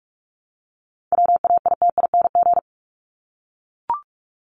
Morse code, heard as a keyed tone, about 44 wpm, 700 Hz. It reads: WAITING
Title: ING Suffix - Top 100 Words with Base Words 45wpm